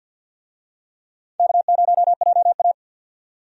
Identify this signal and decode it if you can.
Morse code, heard as a keyed tone, about 50 wpm, 700 Hz. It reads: K0JA
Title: USA Call Signs - 4 Characters - 50wpm